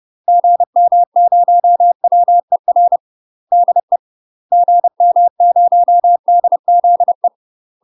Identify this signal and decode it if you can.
Morse code, heard as a keyed tone, about 30 wpm, 700 Hz. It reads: GM0WER DE GM0DZE
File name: UK MASTER QSOs TRAINING / 30WPM / UK MASTER QSOs 0047